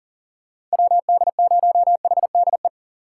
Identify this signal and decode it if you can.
Morse code, heard as a keyed tone, about 39 wpm, 700 Hz. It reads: WD0HDE
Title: USA Call Signs - 6 Characters - 6x Character Spacing - 40wpm